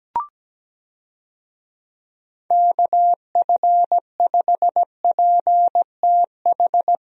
Morse code, heard as a keyed tone, about 17 wpm, 700 Hz. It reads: KF5PTH
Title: USA Call Signs - 6 Characters - Repeated 2x - 17wpm